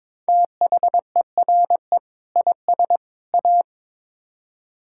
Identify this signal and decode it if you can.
Morse code, heard as a keyed tone, about 22 wpm, 700 Hz. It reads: THERE IS A
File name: practice / 22wpm / Build A Phrase - 5-Word Phrases - 22wpm